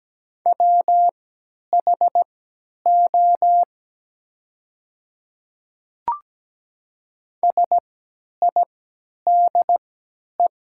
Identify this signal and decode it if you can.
Morse code, heard as a keyed tone, about 17 wpm, spaced as for 10 wpm, 700 Hz. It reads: WHO SIDE
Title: 3x Character Spacing - Top 200 Words - 17wpm